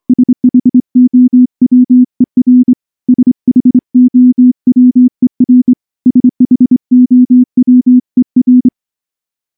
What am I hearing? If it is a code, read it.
Morse code, heard as a keyed tone, about 23 wpm, 250 Hz. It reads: SHOWER SHOWER SHOWER
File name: morse / morse-training-25WPM